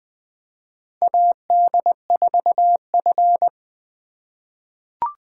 Morse code, heard as a keyed tone, about 20 wpm, 700 Hz. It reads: AD4F
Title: USA Call Signs - 4 Characters - 4x Character Spacing - 20wpm